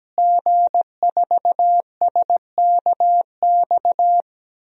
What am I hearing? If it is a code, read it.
Morse code, heard as a keyed tone, about 17 wpm, 700 Hz. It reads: G4SKX
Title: DXCC-100 Call Signs - 5 Characters - Repeated 2x - 17wpm